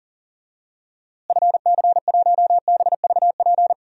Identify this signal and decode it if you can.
Morse code, heard as a keyed tone, about 40 wpm, 700 Hz. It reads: FC1BVP
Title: DXCC-100 Call Signs - 6 Characters - Repeated 3x - 40wpm